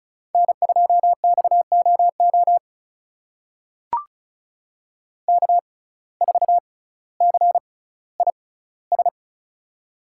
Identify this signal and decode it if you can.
Morse code, heard as a keyed tone, about 36 wpm, spaced as for 18 wpm, 700 Hz. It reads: N2XOO K4CIS
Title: USA Call Signs - 5 Characters - 6x Character Spacing - 35wpm